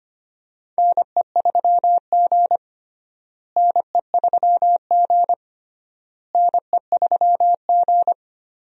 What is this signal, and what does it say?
Morse code, heard as a keyed tone, about 25 wpm, 700 Hz. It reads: NE3G NE3G NE3G
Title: USA Call Signs - 4 Characters - Repeated 3x - 25wpm